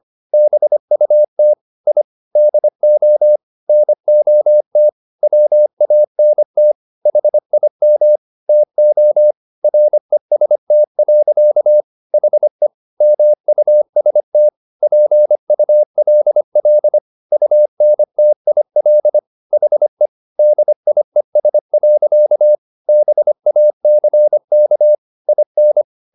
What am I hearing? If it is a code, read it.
Morse code, heard as a keyed tone, about 25 wpm, 600 Hz. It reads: BUT I DO NOT WANT HIM TO REST. HE MUST PULL UNTIL HE DIES. BACK IN